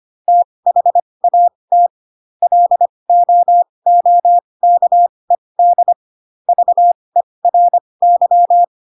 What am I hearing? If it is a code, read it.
Morse code, heard as a keyed tone, about 25 wpm, spaced as for 20 wpm, 700 Hz. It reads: THAT LOOKED VERY